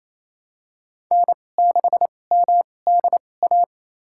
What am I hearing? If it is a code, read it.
Morse code, heard as a keyed tone, about 28 wpm, spaced as for 20 wpm, 700 Hz. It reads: N6MDA
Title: USA Call Signs - 2x Character Spacing - 28wpm